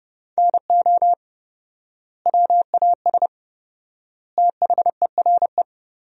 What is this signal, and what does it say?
Morse code, heard as a keyed tone, about 30 wpm, 700 Hz. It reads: NO WAS THERE
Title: 4x Word-Spacing - Sentences from Top 100 Words - 30wpm